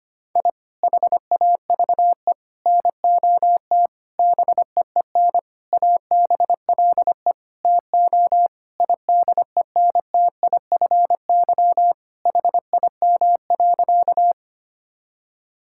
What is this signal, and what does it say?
Morse code, heard as a keyed tone, about 25 wpm, 700 Hz. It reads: I HAVE NOT BEEN ABLE TO IDENTIFY HIM.
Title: Sentences from Top 1000 Words - Review Part 6 - 25wpm